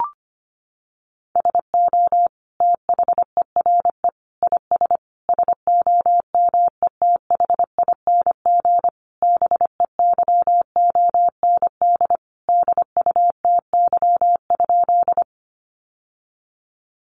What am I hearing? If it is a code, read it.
Morse code, heard as a keyed tone, about 25 wpm, 700 Hz. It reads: SO THERE IS SOMETHING BEYOND DUTY?
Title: Sentences from Top 1500 Words 25wpm